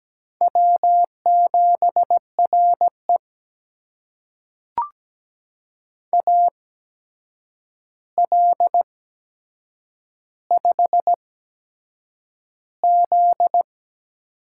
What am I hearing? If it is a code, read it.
Morse code, heard as a keyed tone, about 17 wpm, spaced as for 7 wpm, 700 Hz. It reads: W7RE AL5Z
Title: USA Call Signs - 4 Characters - 8x Character Spacing - 17wpm